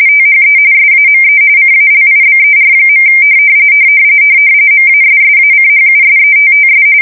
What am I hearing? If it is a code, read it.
RTTY, about 45.45 baud. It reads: THE SAME CALIBER. MASSACHUSETTS